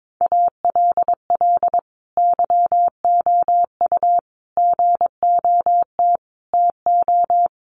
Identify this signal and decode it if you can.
Morse code, heard as a keyed tone, about 22 wpm, 700 Hz. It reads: ALL YOU GOT TO
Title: A Sound Of Thunder 22wpm